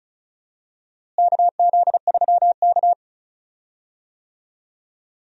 Morse code, heard as a keyed tone, about 36 wpm, 700 Hz. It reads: KZ3K
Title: USA Call Signs - 4 Characters - Repeated 3x - 35wpm